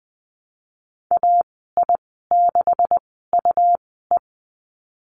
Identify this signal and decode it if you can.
Morse code, heard as a keyed tone, about 20 wpm, spaced as for 15 wpm, 700 Hz. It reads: AI6UE
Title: 186 Lesson 30g - Letter b - Call Signs 20wpm